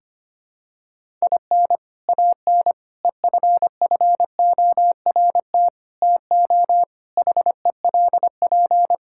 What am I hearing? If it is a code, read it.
Morse code, heard as a keyed tone, about 25 wpm, 700 Hz. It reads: IN AN EFFORT TO HELP
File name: practice / 25wpm / Sentences with 4-Word Phrases - 25wpm